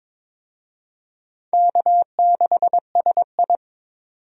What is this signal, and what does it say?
Morse code, heard as a keyed tone, about 22 wpm, 700 Hz. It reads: K6SI
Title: USA Call Signs - 4 Characters - 22wpm